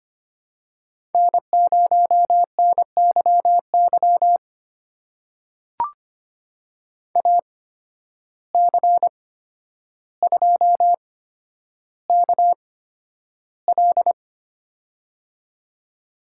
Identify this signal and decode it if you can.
Morse code, heard as a keyed tone, about 25 wpm, spaced as for 11 wpm, 700 Hz. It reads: N0NYY AC2KL
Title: USA Call Signs - 5 Characters - 8x Character Spacing - 25wpm